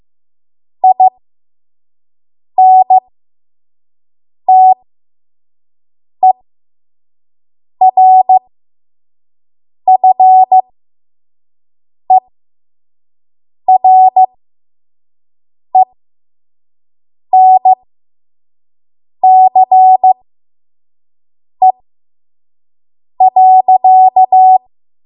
Morse code, heard as a keyed tone, about 15 wpm, spaced as for 5 wpm, 750 Hz. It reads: INTERFERENCE.
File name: Morse / 05WPM